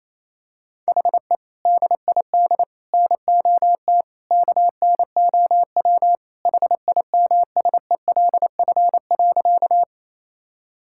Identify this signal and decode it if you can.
Morse code, heard as a keyed tone, about 28 wpm, 700 Hz. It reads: HE DID NOT KNOW HIMSELF.
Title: Sentences from Top 1600 Words 28wpm